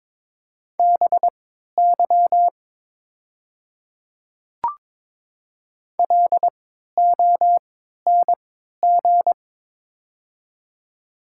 Morse code, heard as a keyed tone, about 22 wpm, spaced as for 13 wpm, 700 Hz. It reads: BY LONG
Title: 3x Character Spacing - Top 100 Words - 22wpm